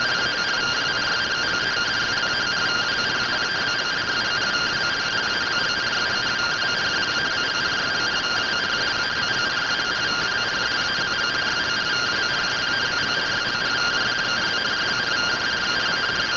Two-tone teleprinter signal, about 170 Hz shift.
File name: findings / SAGITAL fr15 nb203 3mm.gen-20 rtty rx